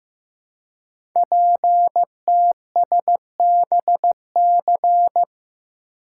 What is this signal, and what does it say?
Morse code, heard as a keyed tone, about 15 wpm, 700 Hz. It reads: PTSBC
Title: Code Groups - 5 Letters - 15wpm